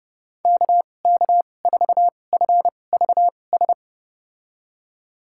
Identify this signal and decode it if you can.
Morse code, heard as a keyed tone, about 30 wpm, spaced as for 22 wpm, 700 Hz. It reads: KK4FVS
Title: USA Call Signs - 6 Characters - 2x Character Spacing - 30wpm